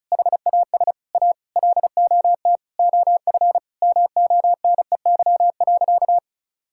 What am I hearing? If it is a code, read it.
Morse code, heard as a keyed tone, about 34 wpm, 700 Hz. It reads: HAS A LOT OF MONEY.
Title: Sentences from Top 1000 Words - Review Part 5 - 35wpm